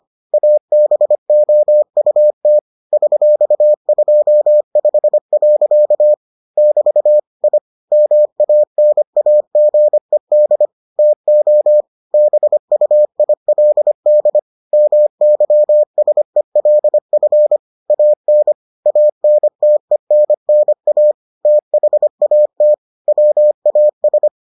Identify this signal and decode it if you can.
Morse code, heard as a keyed tone, about 25 wpm, 600 Hz. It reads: ABOUT $25. <BT> I MANAGED TO BUILD MYSELF AN ANTENNA THAT WAS